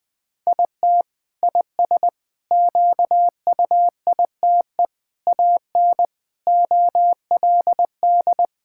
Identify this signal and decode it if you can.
Morse code, heard as a keyed tone, about 20 wpm, 700 Hz. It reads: IT IS QUITE AN OLD
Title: Sentences from Top 1000 Words - Review Part 5 - 20wpm